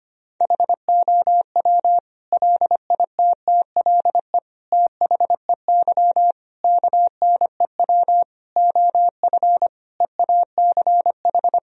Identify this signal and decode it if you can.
Morse code, heard as a keyed tone, about 25 wpm, 700 Hz. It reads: HOW LITTLE THEY KNEW OF EACH
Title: Sentences from Top 1000 Words - Review Part 1 - 25wpm